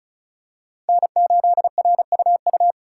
Morse code, heard as a keyed tone, about 36 wpm, 700 Hz. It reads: N8RUU
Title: USA Call Signs - 5 Characters - 2x Character Spacing - 35wpm